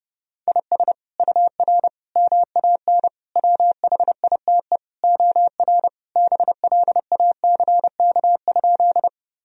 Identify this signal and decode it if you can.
Morse code, heard as a keyed tone, about 30 wpm, 700 Hz. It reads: IS UR MAN WHITE OR BLACK?
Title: Sentences from Top 400 Words and Abbreviations 30wpm